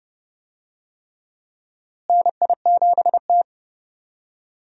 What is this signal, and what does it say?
Morse code, heard as a keyed tone, about 31 wpm, 700 Hz. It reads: NI7T
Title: USA Call Signs - 4 Characters - Repeated 3x - 30wpm